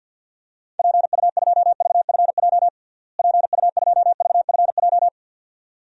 Morse code, heard as a keyed tone, about 52 wpm, 700 Hz. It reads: PU2VFJ PU2VFJ
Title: DXCC-100 Call Signs - 6 Characters - Repeated 2x - 50wpm